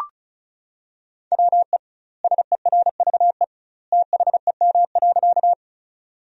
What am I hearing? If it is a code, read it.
Morse code, heard as a keyed tone, about 36 wpm, 700 Hz. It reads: WE SERVE THEM.
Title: 2x Word-Spacing - Sentences from Top 400 Words - 35wpm